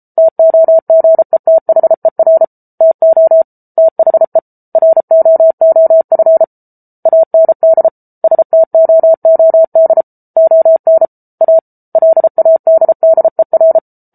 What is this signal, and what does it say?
Morse code, heard as a keyed tone, about 34 wpm, 650 Hz. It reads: TOGETHER TO THE ROOF AND STOOD ON A LADDER